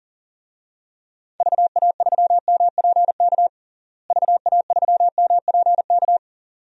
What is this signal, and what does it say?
Morse code, heard as a keyed tone, about 37 wpm, 700 Hz. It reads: VA3MPK VA3MPK
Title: DXCC-100 Call Signs - 6 Characters - Repeated 2x - 40wpm